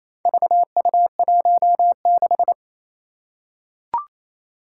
Morse code, heard as a keyed tone, about 28 wpm, 700 Hz. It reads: VU16
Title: Four Challenging Letter-Number Combinations 28wpm